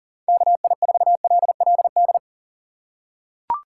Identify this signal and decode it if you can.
Morse code, heard as a keyed tone, about 40 wpm, 700 Hz. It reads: KI4LLD
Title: USA Call Signs - 6 Characters - 4x Character Spacing - 40wpm